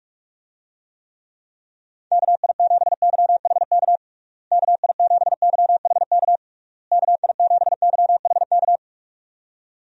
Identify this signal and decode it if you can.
Morse code, heard as a keyed tone, about 46 wpm, 700 Hz. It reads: KI7YHK KI7YHK KI7YHK
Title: USA Call Signs - 6 Characters - Repeated 3x - 45wpm